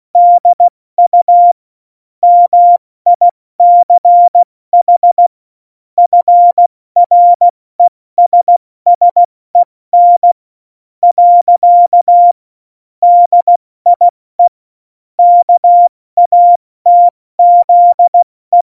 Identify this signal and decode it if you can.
Morse code, heard as a keyed tone, about 16 wpm, spaced as for 14 wpm, 700 Hz. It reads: DU MICH FRESSEN. DIE KATZE
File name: Morsekurs / Witze / Witz 18. September 2018, 70 F80